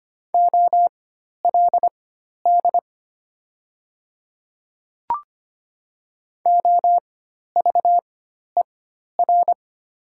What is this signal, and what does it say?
Morse code, heard as a keyed tone, about 25 wpm, spaced as for 12 wpm, 700 Hz. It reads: OLD OVER
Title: 4x Character Spacing - Top 500 Words - 25wpm